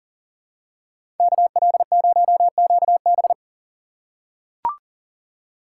Morse code, heard as a keyed tone, about 40 wpm, 700 Hz. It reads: KL0QB
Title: USA Call Signs - 5 Characters - 4x Character Spacing - 40wpm